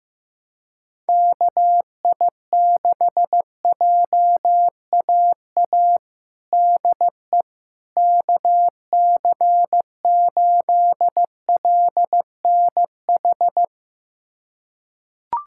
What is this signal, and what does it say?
Morse code, heard as a keyed tone, about 16 wpm, 700 Hz. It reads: KI6JAA DE KC8LNH
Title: USA Callsign DE Callsign 15wpm